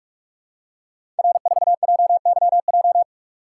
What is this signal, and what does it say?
Morse code, heard as a keyed tone, about 44 wpm, 700 Hz. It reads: R4JYJ
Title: Code Groups - 5 Letter:Numbers - 45wpm